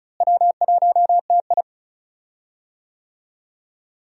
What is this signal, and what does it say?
Morse code, heard as a keyed tone, about 36 wpm, 700 Hz. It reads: W1TI
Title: USA Call Signs - 4 Characters - Repeated 2x - 35wpm